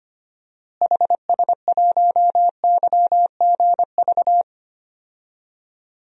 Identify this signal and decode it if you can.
Morse code, heard as a keyed tone, about 25 wpm, 700 Hz. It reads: HS1YGV
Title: DXCC-100 Call Signs - 6 Characters - Repeated 2x - 25wpm